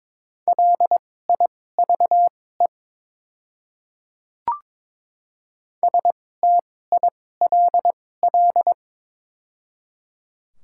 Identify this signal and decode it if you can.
Morse code, heard as a keyed tone, about 22 wpm, spaced as for 16 wpm, 700 Hz. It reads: LIVE STILL
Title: 2x Character Spacing - Top 500 Words - 22wpm